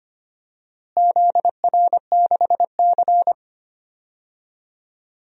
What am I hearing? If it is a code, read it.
Morse code, heard as a keyed tone, about 25 wpm, 700 Hz. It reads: ZR6C
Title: DXCC-100 Call Signs - 4 Characters - Repeated 3x - 25wpm